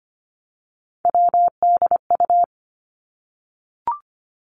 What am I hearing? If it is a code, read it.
Morse code, heard as a keyed tone, about 25 wpm, 700 Hz. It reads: WDU
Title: Three Challenging Letter-Number Combinations 25wpm